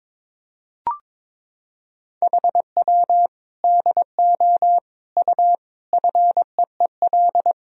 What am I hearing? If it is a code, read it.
Morse code, heard as a keyed tone, about 22 wpm, 700 Hz. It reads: HW DO U FEEL
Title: Sentences from Top 400 Words and Abbreviations 22wpm